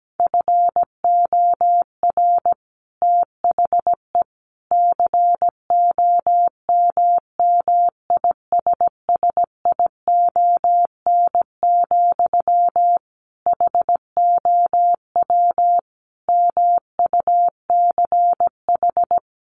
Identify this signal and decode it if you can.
Morse code, heard as a keyed tone, about 17 wpm, 700 Hz. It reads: FOR THE COMMISSION, HOW MUCH
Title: Sentences from Top 1400 Words 17wpm